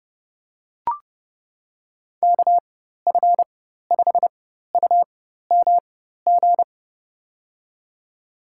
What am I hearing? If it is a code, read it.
Morse code, heard as a keyed tone, about 32 wpm, spaced as for 14 wpm, 700 Hz. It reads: KF5UMG
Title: USA Call Signs - 6 Characters - 4x Character Spacing - 30wpm